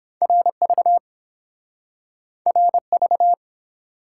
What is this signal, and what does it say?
Morse code, heard as a keyed tone, about 30 wpm, 700 Hz. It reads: RV RV